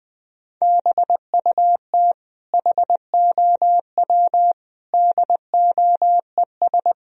Morse code, heard as a keyed tone, about 20 wpm, 700 Hz. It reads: BUT HOW DOES
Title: Sentences from Top 1000 Words - Review Part 3 - 20wpm